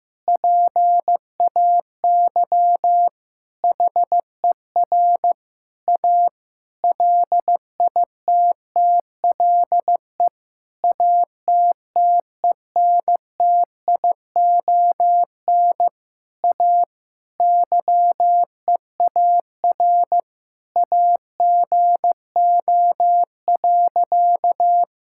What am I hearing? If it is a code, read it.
Morse code, heard as a keyed tone, about 15 wpm, 700 Hz. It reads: PAY HER A LITTLE ATTENTION A YEAR AGO.
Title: Sentences from Top 1000 Words - Review Part 6 - 15wpm